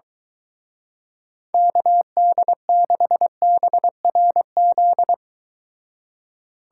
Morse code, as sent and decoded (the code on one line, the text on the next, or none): -.- -.. -.... -... .-. --..
KD6BRZ